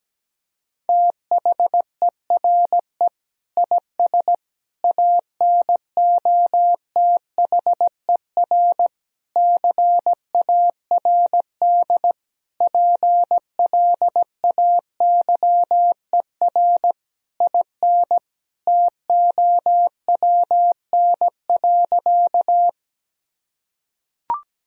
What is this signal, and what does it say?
Morse code, heard as a keyed tone, about 17 wpm, 700 Hz. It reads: THERE IS ANOTHER CARD PLAYER IN TOWN.
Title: Sentences from Top 1000 Words - Review Part 4 - 17wpm